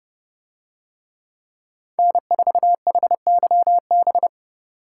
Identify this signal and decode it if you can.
Morse code, heard as a keyed tone, about 29 wpm, 700 Hz. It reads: N4HYB